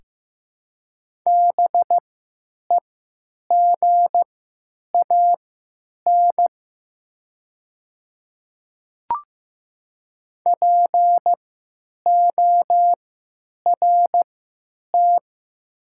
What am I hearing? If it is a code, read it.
Morse code, heard as a keyed tone, about 15 wpm, spaced as for 9 wpm, 700 Hz. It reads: BEGAN PORT
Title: 3x Character Spacing - Top 300 Words - 15wpm